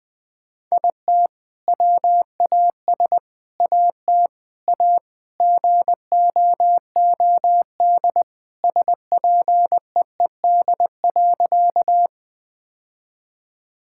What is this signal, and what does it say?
Morse code, heard as a keyed tone, about 20 wpm, 700 Hz. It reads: IT WAS AT A GOOD SPEED.
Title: Sentences from Top 1000 Words - Review Part 5 - 20wpm